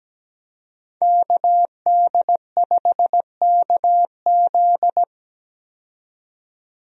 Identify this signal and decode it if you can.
Morse code, heard as a keyed tone, about 17 wpm, 700 Hz. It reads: KD5KZ
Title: USA Call Signs - 5 Characters - 17wpm